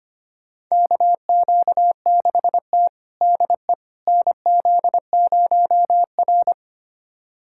Morse code, heard as a keyed tone, about 25 wpm, 700 Hz. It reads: KQ6T DE NZ0R
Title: USA Callsign DE Callsign - 4 Characters - 25wpm